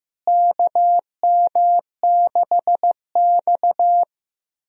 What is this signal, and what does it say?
Morse code, heard as a keyed tone, about 15 wpm, 700 Hz. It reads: KM6X
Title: USA Call Signs - 4 Characters - Repeated 3x - 15wpm